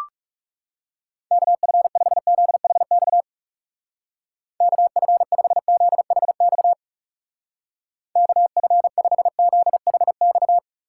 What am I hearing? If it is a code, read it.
Morse code, heard as a keyed tone, about 39 wpm, 700 Hz. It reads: KF5ZHX KF5ZHX KF5ZHX